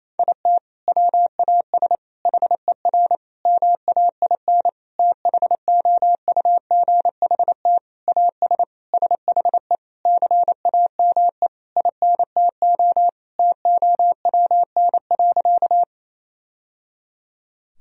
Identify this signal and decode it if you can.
Morse code, heard as a keyed tone, about 28 wpm, 700 Hz. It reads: IT WAS HER MAIN THOUGHT AS SHE CAME INTO TOWN.
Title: Sentences from Top 1000 Words - Review Part 2 - 28wpm